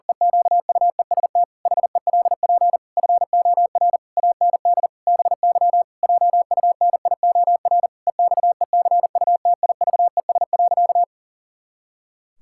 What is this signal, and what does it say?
Morse code, heard as a keyed tone, about 40 wpm, 700 Hz. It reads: EQUEST HELP FOR AND BY JUNIOR EXECUTIVES.
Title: Pangram Challenge v2 - 40wpm